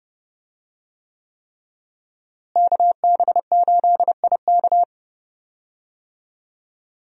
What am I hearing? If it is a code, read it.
Morse code, heard as a keyed tone, about 30 wpm, 700 Hz. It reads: KB8IK
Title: USA Call Signs - 5 Characters - Repeated 3x - 30wpm